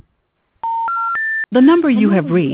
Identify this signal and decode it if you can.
Special information tone, with speech.